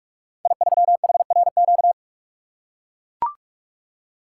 Morse code, heard as a keyed tone, about 44 wpm, 700 Hz. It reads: I3HRQ
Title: Code Groups - 5 Letter:Numbers - 45wpm